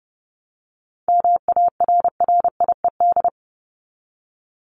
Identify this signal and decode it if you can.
Morse code, heard as a keyed tone, about 30 wpm, 700 Hz. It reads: MARRIED
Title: ED Suffix - Top 100 Words with Base Words 30wpm